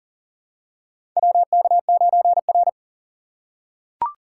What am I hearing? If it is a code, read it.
Morse code, heard as a keyed tone, about 40 wpm, 700 Hz. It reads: WK9R
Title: USA Call Signs - 4 Characters - 2x Character Spacing - 40wpm